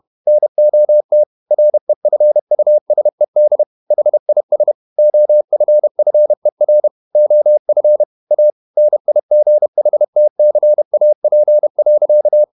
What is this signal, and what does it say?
Morse code, heard as a keyed tone, about 31 wpm, 600 Hz. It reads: NOT REFUSED HIS OFFER OF A NIGHTCAP.
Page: Never 31wpm | CW med Gnister